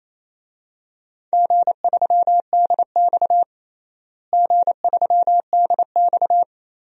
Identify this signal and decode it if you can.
Morse code, heard as a keyed tone, about 28 wpm, 700 Hz. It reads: G3DX G3DX
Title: DXCC-100 Call Signs - 4 Characters - Repeated 2x - 28wpm